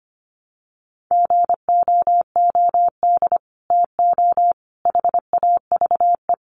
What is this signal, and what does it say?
Morse code, heard as a keyed tone, about 25 wpm, 700 Hz. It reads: GOOD TO HAVE
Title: Sets of 5 Words Encore2 25wpm